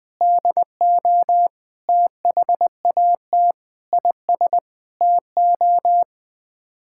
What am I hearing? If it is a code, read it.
Morse code, heard as a keyed tone, about 21 wpm, 700 Hz. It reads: DO THAT IS TO
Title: Sets of 8 Words - 20wpm